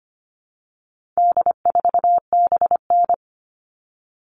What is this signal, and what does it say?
Morse code, heard as a keyed tone, about 25 wpm, 700 Hz. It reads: D4BN
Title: All Cut Numbers - 4-Digit 25wpm